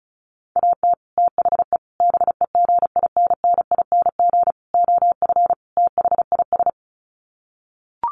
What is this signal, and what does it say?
Morse code, heard as a keyed tone, about 34 wpm, 700 Hz. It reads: AT THE BEGINNING OF THIS
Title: Sets of 5 Words Encore 35wpm